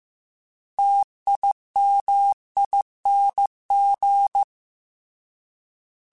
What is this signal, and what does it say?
Morse code, heard as a keyed tone, about 15 wpm, 800 Hz. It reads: TIMING